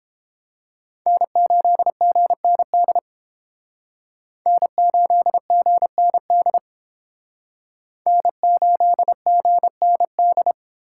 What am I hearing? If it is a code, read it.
Morse code, heard as a keyed tone, about 30 wpm, 700 Hz. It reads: N8GND N8GND N8GND